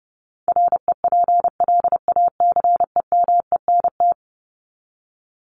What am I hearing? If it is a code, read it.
Morse code, heard as a keyed tone, about 30 wpm, 700 Hz. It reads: REPLACEMENT
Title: Top 1751-2000 Words 30wpm